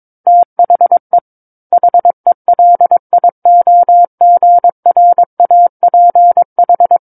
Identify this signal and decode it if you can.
Morse code, heard as a keyed tone, about 22 wpm, 700 Hz. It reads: THE HELIOGRAPH
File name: Book One / 13-Chapter 13 - 22 WPM